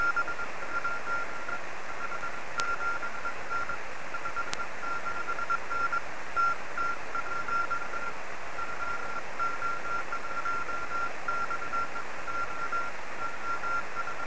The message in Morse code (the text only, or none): none